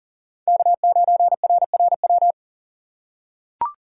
Morse code, heard as a keyed tone, about 40 wpm, 700 Hz. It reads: K9RRW
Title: USA Call Signs - 8x Character Spacing - 40wpm